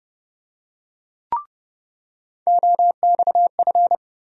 Morse code, heard as a keyed tone, about 32 wpm, 700 Hz. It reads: OXF
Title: USA Call Sign Postfixes - 3 Letters 30wpm